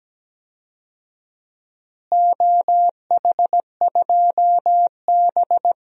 Morse code, heard as a keyed tone, about 17 wpm, 700 Hz. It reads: OH2B
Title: DXCC-100 Call Signs - 4 Characters - Repeated 2x - 17wpm